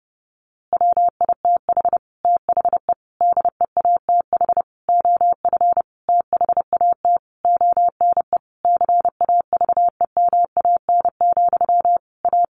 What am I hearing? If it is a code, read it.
Morse code, heard as a keyed tone, about 28 wpm, 700 Hz. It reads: WITH THE DEATH OF THAT ONE CAVEMAN, A
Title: A Sound Of Thunder 30wpm